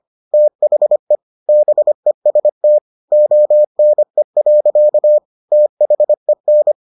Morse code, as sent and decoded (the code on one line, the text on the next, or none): - .... / -... ... - / --- -. .-.-.- / - .... -.
THE BEST ONE. THEN